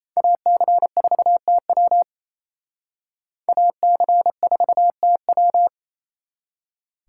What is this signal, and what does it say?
Morse code, heard as a keyed tone, about 36 wpm, 700 Hz. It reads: AC4TW AC4TW